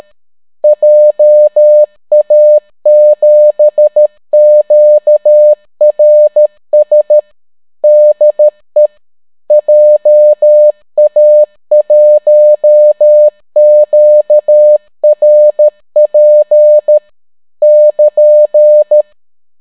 Morse code, heard as a keyed tone, about 13 wpm, 600 Hz. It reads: JA7QRS DE JA1QRP <KN>